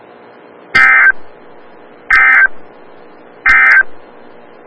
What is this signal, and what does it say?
Emergency Alert System (SAME) tones, heard as harsh buzzing bursts. It is heard from 0.6 to 3.9 s.
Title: Des Moines, IA NOAA Weather Radio - WXL57